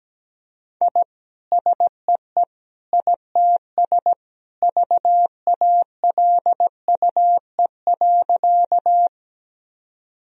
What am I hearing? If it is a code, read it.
Morse code, heard as a keyed tone, about 17 wpm, 700 Hz. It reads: I SEE ITS VALUE.
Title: Sentences from Top 1000 Words - Review Part 3 - 17wpm